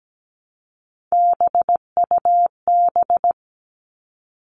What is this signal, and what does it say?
Morse code, heard as a keyed tone, about 17 wpm, 700 Hz. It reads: BUB
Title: All Cut Numbers - 3-Digit 17wpm